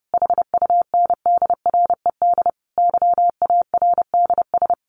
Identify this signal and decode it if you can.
Morse code, heard as a keyed tone, about 30 wpm, 700 Hz. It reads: HUNDRED YARDS
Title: A Sound Of Thunder 30wpm